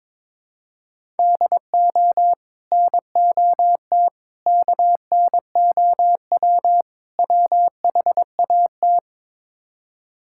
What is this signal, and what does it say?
Morse code, heard as a keyed tone, about 22 wpm, 700 Hz. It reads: DO NOT KNOW WHAT
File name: practice / 22wpm / Build A Phrase - 5-Word Phrases - 22wpm